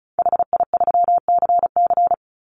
Morse code, heard as a keyed tone, about 34 wpm, 700 Hz. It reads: HI3CC
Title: Call Signs - 35wpm